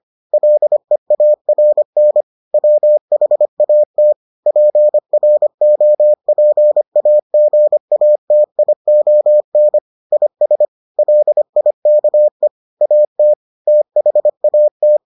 Morse code, heard as a keyed tone, about 25 wpm, 600 Hz. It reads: LEARN WHAT PROPAGATION IS LIKE AT THAT